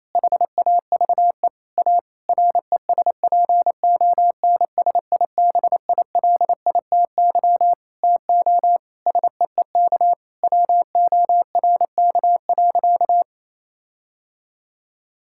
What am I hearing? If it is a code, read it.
Morse code, heard as a keyed tone, about 28 wpm, 700 Hz. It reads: HAVE A RESPONSIBILITY TO SEEK WORK.
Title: Sentences from Top 1600 Words 28wpm